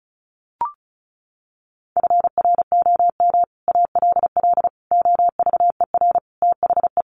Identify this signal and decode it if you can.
Morse code, heard as a keyed tone, about 34 wpm, 700 Hz. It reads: FROM ALL OVER THE
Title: Sets of 5 Words 35wpm